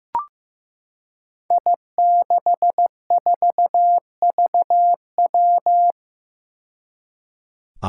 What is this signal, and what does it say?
Morse code, heard as a keyed tone, about 15 wpm, 700 Hz. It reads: I64VW
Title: Five Challenging Letter-Number Combinations 15wpm